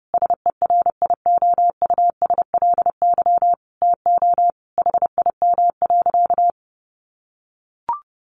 Morse code, heard as a keyed tone, about 28 wpm, 700 Hz. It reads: SERIOUSLY TO HIM.
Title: Sentences from Top 1200 Words 30wpm